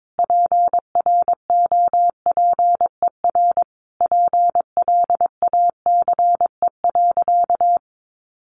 Morse code, heard as a keyed tone, about 22 wpm, 700 Hz. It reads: PROPER PLACE.
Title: Sentences from Top 800 Words 22wpm